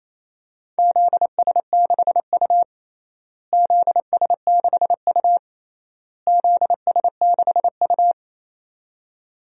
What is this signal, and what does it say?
Morse code, heard as a keyed tone, about 28 wpm, 700 Hz. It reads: ZS6U ZS6U ZS6U
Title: DXCC-100 Call Signs - 4 Characters - Repeated 3x - 28wpm